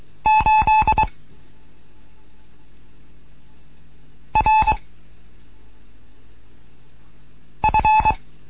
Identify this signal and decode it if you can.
Morse code, heard as a keyed tone, about 23 wpm, 870 Hz. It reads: 8 R F